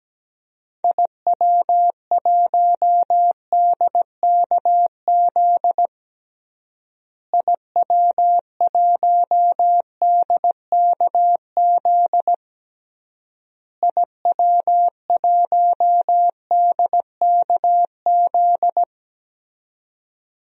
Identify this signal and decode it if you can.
Morse code, heard as a keyed tone, about 17 wpm, 700 Hz. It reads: IW1DKZ IW1DKZ IW1DKZ
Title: DXCC-100 Call Signs - 6 Characters - Repeated 3x - 17wpm